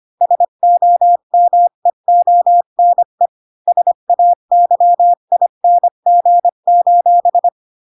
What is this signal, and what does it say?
Morse code, heard as a keyed tone, about 25 wpm, 700 Hz. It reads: SOMEONE SAYING:
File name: Pinocchio / Pinocchio ch4 20 25 ik1hge